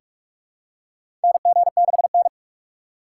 Morse code, heard as a keyed tone, about 44 wpm, 700 Hz. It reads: NG6N
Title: USA Call Signs - 4 Characters - 2x Character Spacing - 45wpm